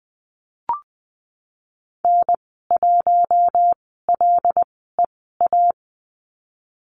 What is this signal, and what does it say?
Morse code, heard as a keyed tone, about 20 wpm, spaced as for 15 wpm, 700 Hz. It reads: N1LEA
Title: 147 Lesson 24g - Letter y - Call Signs 20wpm